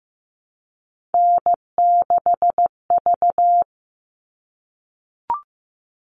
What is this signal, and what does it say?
Morse code, heard as a keyed tone, about 15 wpm, 700 Hz. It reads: N6V
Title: All Cut Numbers - 3-Digit 15wpm